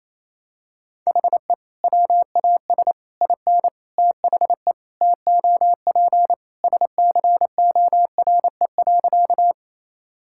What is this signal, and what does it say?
Morse code, heard as a keyed tone, about 27 wpm, 700 Hz. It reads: HE WAS IN THE TOP SCORE.
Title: Sentences from Top 1600 Words 28wpm